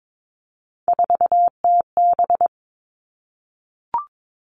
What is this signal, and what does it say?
Morse code, heard as a keyed tone, about 22 wpm, 700 Hz. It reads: 4TB
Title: Three Challenging Letter-Number Combinations 22wpm